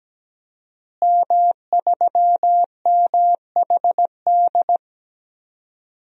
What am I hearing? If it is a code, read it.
Morse code, heard as a keyed tone, about 17 wpm, 700 Hz. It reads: M3MHD
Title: DXCC-100 Call Signs - 5 Characters - Repeated 3x - 17wpm